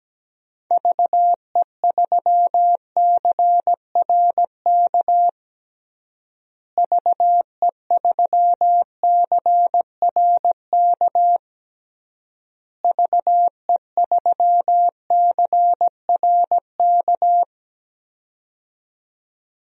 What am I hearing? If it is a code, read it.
Morse code, heard as a keyed tone, about 17 wpm, 700 Hz. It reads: VE3CRK VE3CRK VE3CRK
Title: DXCC-100 Call Signs - 6 Characters - Repeated 3x - 17wpm